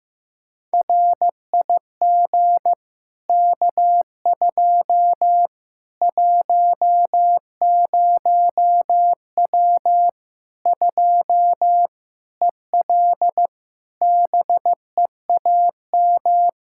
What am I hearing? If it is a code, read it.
Morse code, heard as a keyed tone, about 16 wpm, 700 Hz. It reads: RIG K2 10W 2 EL BEAM
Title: Practice QSO - Exchange 2 - Info - 15wpm